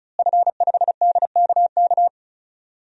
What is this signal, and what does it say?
Morse code, heard as a keyed tone, about 34 wpm, 700 Hz. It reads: F5DKK
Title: DXCC-100 Call Signs - 5 Characters - Repeated 3x - 35wpm